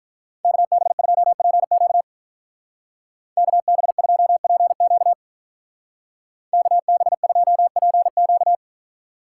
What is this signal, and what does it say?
Morse code, heard as a keyed tone, about 43 wpm, 700 Hz. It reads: KB2PQ KB2PQ KB2PQ
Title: Speed Racing - USA 2x2 Call Signs 35wpm